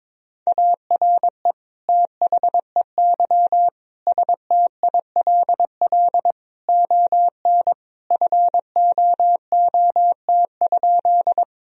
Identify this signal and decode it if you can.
Morse code, heard as a keyed tone, about 22 wpm, 700 Hz. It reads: ARE THEY STILL ON FOOT?
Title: Sentences from Top 1000 Words - Review Part 3 - 22wpm